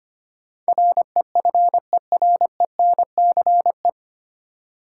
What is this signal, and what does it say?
Morse code, heard as a keyed tone, about 25 wpm, 700 Hz. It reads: REFERENCE
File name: practice / 25wpm / 9 Letter Words - First 3 Letters Given - 25wpm